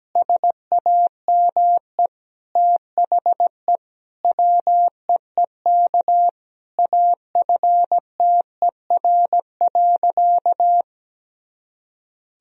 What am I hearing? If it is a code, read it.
Morse code, heard as a keyed tone, about 17 wpm, 700 Hz. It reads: SAME THE WEEK AFTER.
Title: Sentences from Top 500 Words and Abbreviations 17wpm